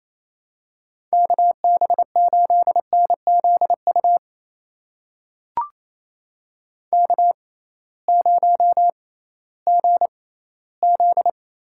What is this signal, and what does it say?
Morse code, heard as a keyed tone, about 28 wpm, spaced as for 16 wpm, 700 Hz. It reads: KB8NZU K0GZ
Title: USA Call Signs - 6x Character Spacing - 28wpm